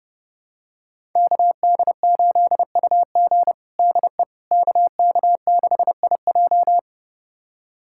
Morse code, heard as a keyed tone, about 30 wpm, 700 Hz. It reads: KD8UG DE KK6IJ
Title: USA Callsign DE Callsign - 5 Characters - 30wpm